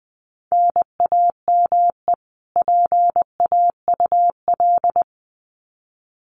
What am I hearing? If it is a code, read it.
Morse code, heard as a keyed tone, about 20 wpm, 700 Hz. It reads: NAME PAUL
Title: Common QSO Phrases 20wpm